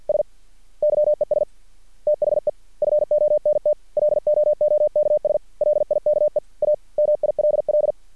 Morse code, heard as a keyed tone, about 50 wpm, 600 Hz. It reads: S YES THE FONT LOOKS LIKE A MILL
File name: archived / CRSnet